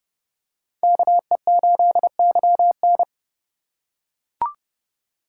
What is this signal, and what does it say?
Morse code, heard as a keyed tone, about 28 wpm, 700 Hz. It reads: KE8YN